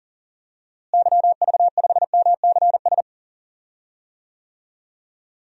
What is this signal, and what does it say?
Morse code, heard as a keyed tone, about 40 wpm, 700 Hz. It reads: YV5MCS
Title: DXCC-100 Call Signs - 6 Characters - Repeated 2x - 40wpm